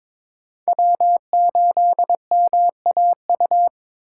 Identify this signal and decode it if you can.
Morse code, heard as a keyed tone, about 22 wpm, 700 Hz. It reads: W8MAU
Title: USA Call Signs - 5 Characters - 2x Character Spacing - 22wpm